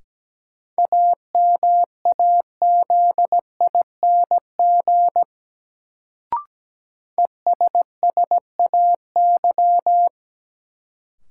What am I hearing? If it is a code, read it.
Morse code, heard as a keyed tone, about 17 wpm, 700 Hz. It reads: AMAZING ESSAY
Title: Sentences from Top 1500 Words - New Word Drill - 17wpm